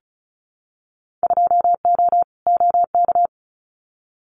Morse code, heard as a keyed tone, about 34 wpm, 700 Hz. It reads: 2O OK
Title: Winter Field Day Exchanges 35wpm